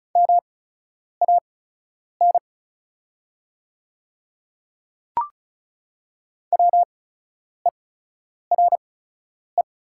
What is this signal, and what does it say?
Morse code, heard as a keyed tone, about 36 wpm, spaced as for 10 wpm, 690 Hz. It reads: MAN WERE
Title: 8x Character Spacing - Top 200 Words - 35wpm